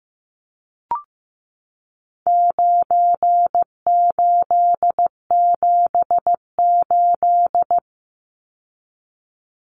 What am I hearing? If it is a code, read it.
Morse code, heard as a keyed tone, about 15 wpm, 700 Hz. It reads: 9878
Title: Numbers - 4-Digit 15wpm